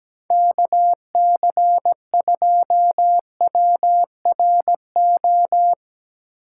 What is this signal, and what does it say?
Morse code, heard as a keyed tone, about 18 wpm, 700 Hz. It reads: KC2WRO
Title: Speed Racing - USA 2x3 Call Signs 17wpm